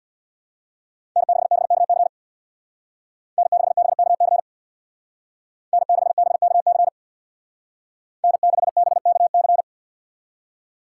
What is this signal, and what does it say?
Morse code, heard as a keyed tone, about 59 wpm, 700 Hz. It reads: N6BKC N6BKC N6BKC N6BKC
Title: Speed Racing - USA Call Signs 50wpm